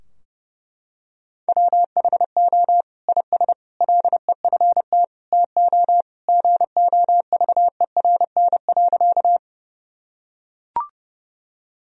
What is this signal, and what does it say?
Morse code, heard as a keyed tone, about 30 wpm, 700 Hz. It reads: WHO IS LEFT TO GOVERN.
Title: Sentences from Top 1000 Words - Review Part 2 - 30wpm